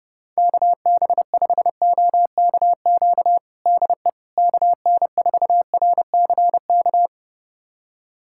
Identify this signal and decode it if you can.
Morse code, heard as a keyed tone, about 32 wpm, 700 Hz. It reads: KB5OKQ DE KN4RCK
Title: USA Callsign DE Callsign 30wpm